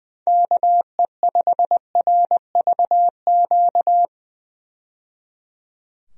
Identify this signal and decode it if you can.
Morse code, heard as a keyed tone, about 20 wpm, 700 Hz. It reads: KE5RVQ
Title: USA Call Signs - 6 Characters - 20wpm